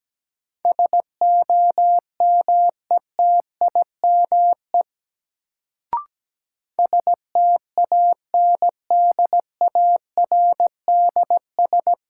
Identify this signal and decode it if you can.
Morse code, heard as a keyed tone, about 17 wpm, 700 Hz. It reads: SOMETIME STANDARDS
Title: Sentences from Top 1000 Words - Review Part 6 - New Word Drill - 17wpm